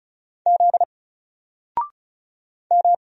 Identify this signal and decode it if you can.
Morse code, heard as a keyed tone, about 36 wpm, 700 Hz. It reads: Z M
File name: Single-Letter-Number-35wpm